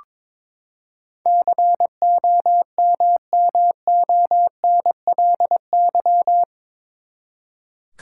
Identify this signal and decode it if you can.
Morse code, heard as a keyed tone, about 22 wpm, 700 Hz. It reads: COMMONLY
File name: practice / 22wpm / Eight Letter Words - 22wpm